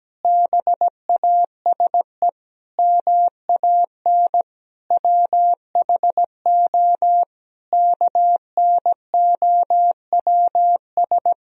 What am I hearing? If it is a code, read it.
Morse code, heard as a keyed tone, about 17 wpm, 700 Hz. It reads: BASE MAN WHO KNOWS
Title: Sentences from Top 1000 Words - Review Part 1 - 17wpm